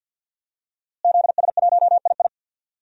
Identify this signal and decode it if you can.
Morse code, heard as a keyed tone, about 52 wpm, 700 Hz. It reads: ZS1II
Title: DXCC-100 Call Signs - 5 Characters - Repeated 3x - 50wpm